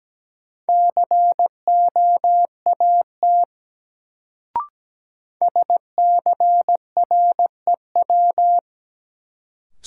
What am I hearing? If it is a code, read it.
Morse code, heard as a keyed tone, about 17 wpm, 700 Hz. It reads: COAT SCREW
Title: Sentences from Top 1600 Words - New Word Drill Alt - 17wpm